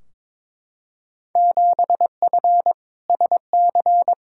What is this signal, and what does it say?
Morse code, heard as a keyed tone, about 22 wpm, 700 Hz. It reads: 7F SC
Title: Field Day Exchanges Encore - 22wpm